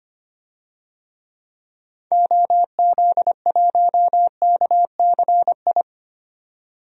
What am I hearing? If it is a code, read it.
Morse code, heard as a keyed tone, about 25 wpm, 700 Hz. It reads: OZ1KCI
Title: DXCC-100 Call Signs - 6 Characters - 25wpm